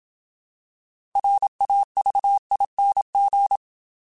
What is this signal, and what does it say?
Morse code, heard as a keyed tone, about 27 wpm, 800 Hz. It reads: RAVING
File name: Alphabetical / Alphabetical mp3s / Rs @25